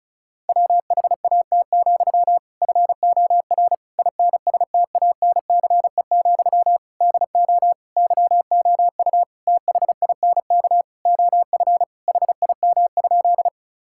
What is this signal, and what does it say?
Morse code, heard as a keyed tone, about 34 wpm, 700 Hz. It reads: WHAT, FOR INSTANCE, DO YOU THINK OF HIM?
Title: Sentences from Top 1000 Words - Review Part 5 - 35wpm